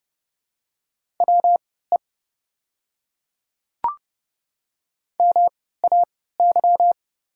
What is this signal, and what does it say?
Morse code, heard as a keyed tone, about 30 wpm, spaced as for 17 wpm, 700 Hz. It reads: WE MAY
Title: 3x Character Spacing - Top 200 Words - 30wpm